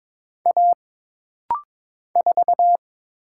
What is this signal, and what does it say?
Morse code, heard as a keyed tone, about 22 wpm, 700 Hz. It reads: A 4
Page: This rapid-fire timing is geared to push your ICR (Instant Character Recognition) skills.